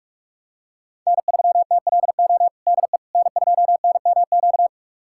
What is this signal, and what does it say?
Morse code, heard as a keyed tone, about 46 wpm, 700 Hz. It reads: N3TLO DE N2NMQ
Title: USA Callsign DE Callsign - 5 Characters - 45wpm